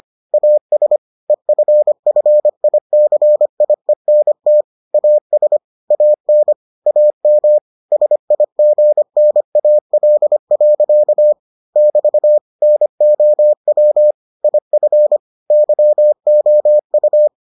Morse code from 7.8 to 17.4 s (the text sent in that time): SIGNAL. <BT> NOW IF YOU